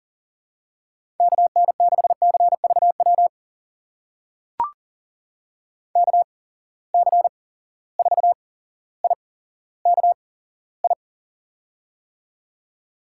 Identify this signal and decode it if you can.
Morse code, heard as a keyed tone, about 40 wpm, spaced as for 17 wpm, 700 Hz. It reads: KN6CVW KC4IKI
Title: USA Call Signs - 6 Characters - 8x Character Spacing - 40wpm